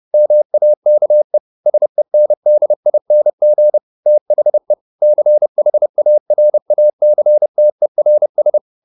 Morse code, heard as a keyed tone, about 30 wpm, 600 Hz. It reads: MAKE SENDING THE CHARACTERS